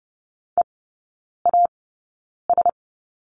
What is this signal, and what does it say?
Morse code, heard as a keyed tone, about 30 wpm, spaced as for 9 wpm, 690 Hz. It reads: EAS
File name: Lesson 16e - Letter c - Three Characters 30wpm